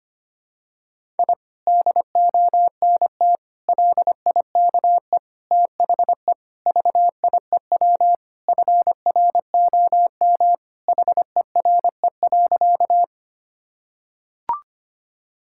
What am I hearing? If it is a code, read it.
Morse code, heard as a keyed tone, about 25 wpm, 700 Hz. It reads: I DONT LIKE THE VIEW FROM HERE.
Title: Sentences from Top 1000 Words - Review Part 6 - 25wpm